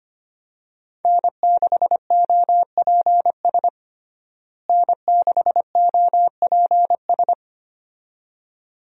Morse code, heard as a keyed tone, about 25 wpm, 700 Hz. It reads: N6OPS N6OPS
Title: USA Call Signs - 5 Characters - Repeated 2x - 25wpm